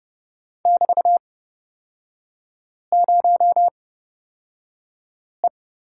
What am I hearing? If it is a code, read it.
Morse code, heard as a keyed tone, about 30 wpm, spaced as for 5 wpm, 700 Hz. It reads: <BT>0E
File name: Lesson 36e - BT - Three Characters 30wpm